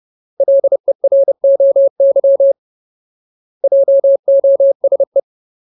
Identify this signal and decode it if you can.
Morse code, heard as a keyed tone, about 30 wpm, 550 Hz. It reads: LEROY JOSE
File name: Ultimate Word List / 30WPM Ultimate